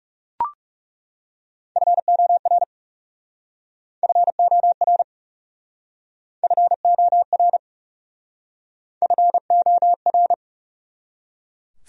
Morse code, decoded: FOR FOR FOR FOR

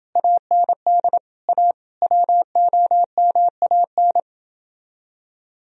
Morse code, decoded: AND A WOMAN

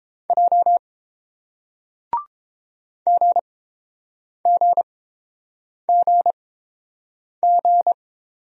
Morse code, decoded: J GGGG

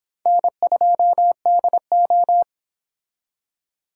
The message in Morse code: N2DO

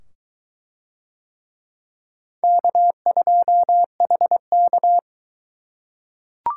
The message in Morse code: K2HK

